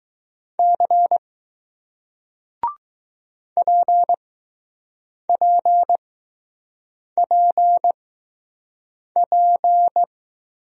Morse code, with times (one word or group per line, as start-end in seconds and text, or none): 0.6-1.2 C
3.6-10.0 PPPP